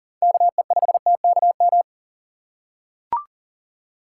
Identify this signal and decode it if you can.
Morse code, heard as a keyed tone, about 40 wpm, 700 Hz. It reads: KE5TKM
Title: USA Call Signs - 6 Characters - 3x Character Spacing - 40wpm